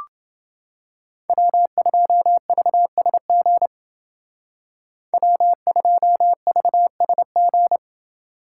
Morse code, heard as a keyed tone, about 29 wpm, 700 Hz. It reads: W2VSG W2VSG